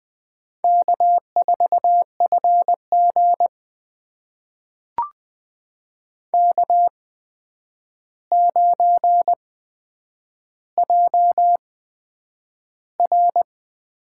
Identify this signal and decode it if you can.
Morse code, heard as a keyed tone, about 20 wpm, spaced as for 9 wpm, 700 Hz. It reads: K4FG K9JR
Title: USA Call Signs - 4 Characters - 8x Character Spacing - 20wpm